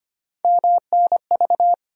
Morse code, heard as a keyed tone, about 25 wpm, 700 Hz. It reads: MNV
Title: USA Call Sign Postfixes - 3 Letters 25wpm